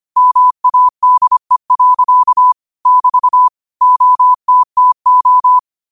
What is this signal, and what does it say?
Morse code, heard as a keyed tone, about 25 wpm, 1000 Hz. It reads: MADE. <BT> OTTO
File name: qotdfiles / 25wpm